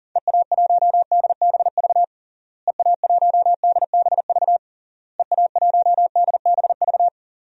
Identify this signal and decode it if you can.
Morse code, heard as a keyed tone, about 37 wpm, 700 Hz. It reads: EA1DBV EA1DBV EA1DBV
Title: DXCC-100 Call Signs - 6 Characters - Repeated 3x - 40wpm